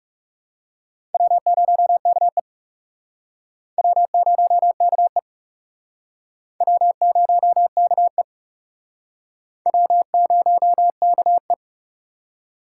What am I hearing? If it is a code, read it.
Morse code, heard as a keyed tone, about 38 wpm, 700 Hz. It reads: W0KE W0KE W0KE W0KE
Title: Speed Racing - USA 1x2 Call Signs 30wpm